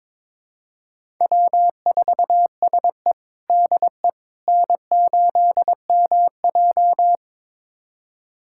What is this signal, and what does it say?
Morse code, heard as a keyed tone, about 21 wpm, 700 Hz. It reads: W4SE DE N8MJ
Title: USA Callsign DE Callsign - 4 Characters - 22wpm